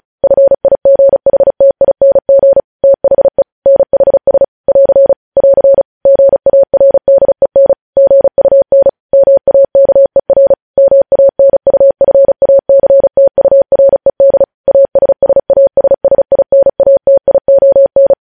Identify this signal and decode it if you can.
Morse code, heard as a keyed tone, about 36 wpm, 550 Hz. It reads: FIGHTING THE NHS <AR> <AR> GARDEN GUN MAKER MANUFACTURED ASSASSINATION
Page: news headlines in morse code at 35 wpm